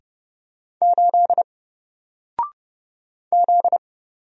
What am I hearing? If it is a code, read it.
Morse code, heard as a keyed tone, about 30 wpm, 700 Hz. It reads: 8 Z
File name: Single-Letter-Number-30wpm